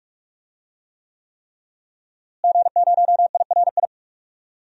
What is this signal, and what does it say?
Morse code, heard as a keyed tone, about 44 wpm, 700 Hz. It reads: G0IRI
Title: DXCC-100 Call Signs - 5 Characters - Repeated 3x - 45wpm